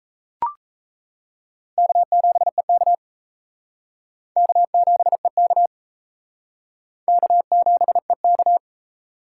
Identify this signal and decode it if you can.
Morse code, heard as a keyed tone, about 34 wpm, 700 Hz. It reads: K7EK K7EK K7EK
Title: Speed Racing - USA 1x2 Call Signs 28wpm